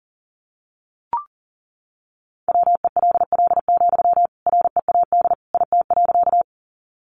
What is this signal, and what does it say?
Morse code, heard as a keyed tone, about 39 wpm, 700 Hz. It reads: WELL, READ IT.
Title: Sentences from Top 200 Words 40wpm